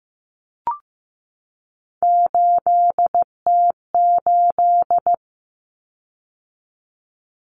Morse code, heard as a keyed tone, about 15 wpm, 700 Hz. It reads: 8T8
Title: Common Cut Numbers - 3-Digit 15wpm